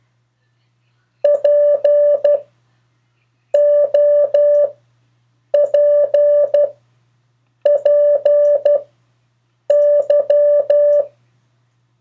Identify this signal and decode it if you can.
Morse code, heard as a keyed tone, about 12 wpm, 580 Hz. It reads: P O P P Y
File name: 3of4-words-2